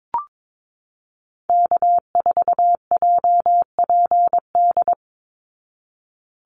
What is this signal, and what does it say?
Morse code, heard as a keyed tone, about 22 wpm, 700 Hz. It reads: K4JPD
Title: Call Signs Repeated 2x - 22wpm